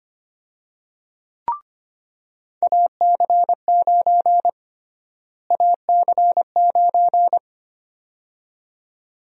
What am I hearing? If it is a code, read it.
Morse code, heard as a keyed tone, about 25 wpm, 700 Hz. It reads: AC9 AC9
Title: USA Call Sign Prefixes - Repeated 2x - 25wpm